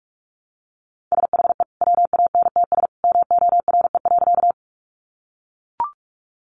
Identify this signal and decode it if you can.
Morse code, heard as a keyed tone, about 44 wpm, 700 Hz. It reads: SHE WANTS MORE.
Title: Easy Phrases 45wpm